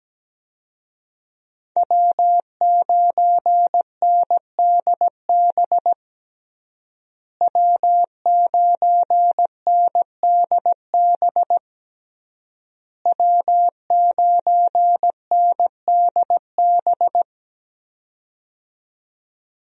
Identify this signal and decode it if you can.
Morse code, heard as a keyed tone, about 17 wpm, 700 Hz. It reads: W9NDB W9NDB W9NDB
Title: USA Call Signs - 5 Characters - Repeated 3x - 17wpm